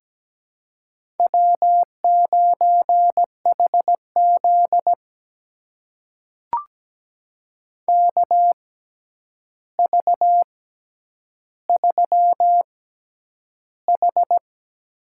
Morse code, heard as a keyed tone, about 17 wpm, spaced as for 9 wpm, 700 Hz. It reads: W9HZ KV3H
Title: USA Call Signs - 4 Characters - 6x Character Spacing - 17wpm